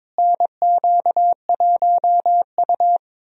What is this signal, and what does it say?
Morse code, heard as a keyed tone, about 21 wpm, 700 Hz. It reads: NQ1U
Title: USA Call Signs - 4 Characters - Repeated 2x - 22wpm